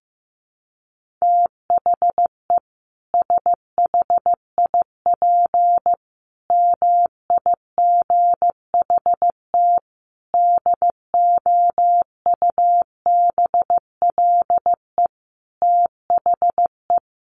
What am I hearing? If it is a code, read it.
Morse code, heard as a keyed tone, about 15 wpm, 700 Hz. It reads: THE SHIP MIGHT DOUBLE THE
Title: Sentences from Top 1400 Words 15wpm